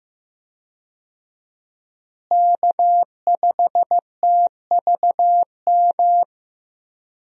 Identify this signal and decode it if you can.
Morse code, heard as a keyed tone, about 15 wpm, 700 Hz. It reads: K5TVM
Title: USA Call Signs - 5 Characters - Repeated 2x - 15wpm